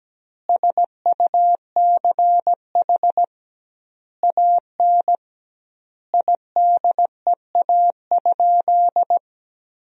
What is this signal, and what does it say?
Morse code, heard as a keyed tone, about 17 wpm, 700 Hz. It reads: SUCH AN IDEA?
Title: 2x Word-Spacing - Sentences from Top 400 Words - 17wpm